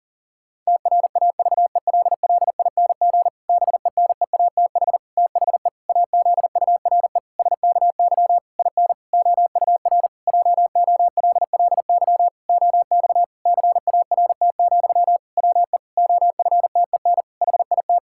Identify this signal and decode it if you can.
Morse code, heard as a keyed tone, about 40 wpm, 700 Hz. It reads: TRAVELLING BENEATH THE AZURE SKY IN OUR JOLLY OX CART, WE OFTEN HIT
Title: Pangram Challenge v2 - 40wpm